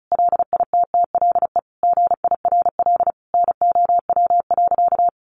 Morse code, heard as a keyed tone, about 34 wpm, 700 Hz. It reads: LITTLE GIRL NOW.
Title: Sentences from Top 1100 Words 35wpm